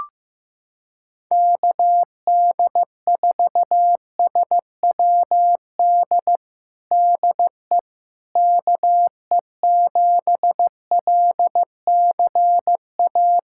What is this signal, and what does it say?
Morse code, heard as a keyed tone, about 15 wpm, 700 Hz. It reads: KD4SWD DE KE7LCA
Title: USA Callsign DE Callsign - 6 Characters - 15wpm